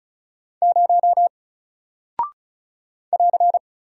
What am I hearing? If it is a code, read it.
Morse code, heard as a keyed tone, about 36 wpm, 700 Hz. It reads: 0 <AR>
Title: Single Character (Letter:Number:Punctuation:Prosign) Alt 35wpm